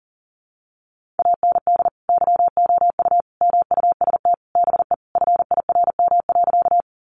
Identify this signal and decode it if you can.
Morse code, heard as a keyed tone, about 40 wpm, 700 Hz. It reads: AND YOU MUST BE FIRM.
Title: Sentences from Top 1400 Words 40wpm